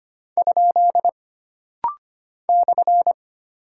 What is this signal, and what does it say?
Morse code, heard as a keyed tone, about 25 wpm, 700 Hz. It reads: ? /
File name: Single Punctuation-Prosign - Rapid-Fire - 25wpm